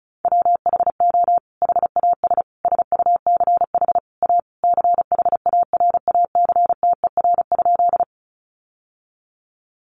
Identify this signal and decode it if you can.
Morse code, heard as a keyed tone, about 34 wpm, 700 Hz. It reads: WHO HAS SUCH A CHARACTER?
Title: Sentences from Top 1200 Words 35wpm